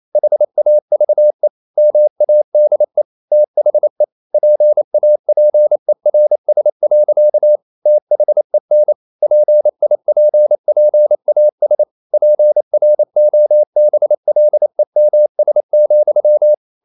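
Morse code, heard as a keyed tone, about 28 wpm, 600 Hz. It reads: HAVE MADE THE PAPERS. THEN PIPPAS PROBLEMS,
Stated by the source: Morse code practice at 28 wpm, text of Ken Follett's novel Never in English, from CW med Gnister.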